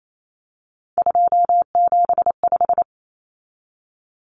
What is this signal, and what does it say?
Morse code, heard as a keyed tone, about 28 wpm, 700 Hz. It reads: 275
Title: Common Cut Numbers - 3-Digit 28wpm